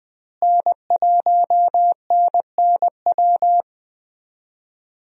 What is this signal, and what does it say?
Morse code, heard as a keyed tone, about 21 wpm, 700 Hz. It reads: N1NNW
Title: USA Call Signs - 5 Characters - Repeated 3x - 20wpm